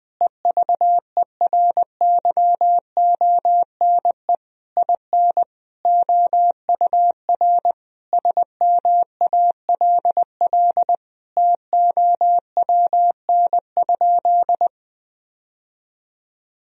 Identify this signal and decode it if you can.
Morse code, heard as a keyed tone, about 20 wpm, 700 Hz. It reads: EVERYONE IN OUR SMALL TOWN?
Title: Sentences from Top 1000 Words - Review Part 4 - 20wpm